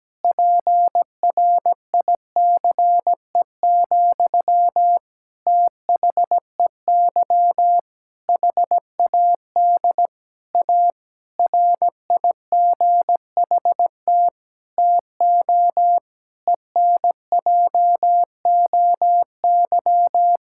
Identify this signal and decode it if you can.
Morse code, heard as a keyed tone, about 17 wpm, 700 Hz. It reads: PRICE, THEY HAD A RIGHT TO ENJOY
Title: Sentences from Top 1000 Words - Review Part 5 - 17wpm